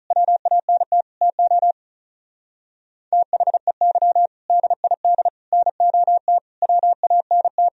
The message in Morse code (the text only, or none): WANT TO THEY DID NOT WANT